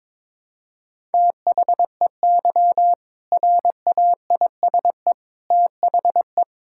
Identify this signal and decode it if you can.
Morse code, heard as a keyed tone, about 22 wpm, 700 Hz. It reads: THEY RAISE THE
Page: morse-code-ninja-practice-22wpm